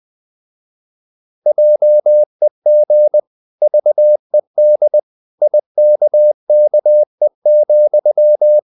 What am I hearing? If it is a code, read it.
Morse code, heard as a keyed tone, about 20 wpm, 600 Hz. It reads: JEG VED IKKE,
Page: Ferie 20 wpm